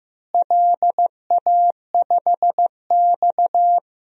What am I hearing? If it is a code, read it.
Morse code, heard as a keyed tone, about 16 wpm, 700 Hz. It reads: LA5X
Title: DXCC-100 Call Signs - 4 Characters - Repeated 3x - 15wpm